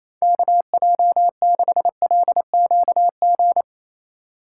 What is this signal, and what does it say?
Morse code, heard as a keyed tone, about 28 wpm, 700 Hz. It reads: KJ6LQG
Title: USA Call Signs - 6 Characters - 3x Character Spacing - 28wpm